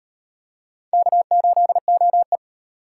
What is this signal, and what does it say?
Morse code, heard as a keyed tone, about 38 wpm, 700 Hz. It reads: K8OE